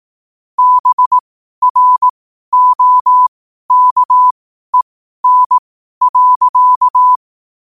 Morse code, heard as a keyed tone, about 18 wpm, spaced as for 12 wpm, 1000 Hz. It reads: BROKEN.
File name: qotdfiles / 12wpm